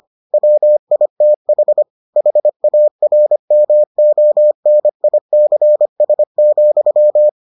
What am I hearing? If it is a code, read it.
Morse code, heard as a keyed tone, about 25 wpm, 600 Hz. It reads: WITH HARMONICS,